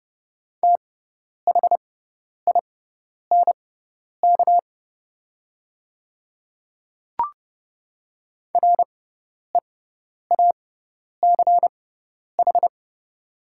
Morse code, heard as a keyed tone, about 30 wpm, spaced as for 10 wpm, 700 Hz. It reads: THINK REACH
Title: 6x Character Spacing - Top 500 Words - 30wpm